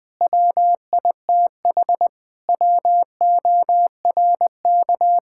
Morse code, heard as a keyed tone, about 21 wpm, 700 Hz. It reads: WITH WORK
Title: Sentences from Top 1000 Words - Review Part 6 - 20wpm